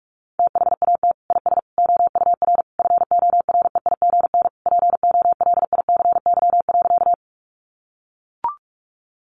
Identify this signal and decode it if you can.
Morse code, heard as a keyed tone, about 44 wpm, 700 Hz. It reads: THAT IS OUR FOREIGN POLICY.
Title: Sentences from Top 1400 Words 45wpm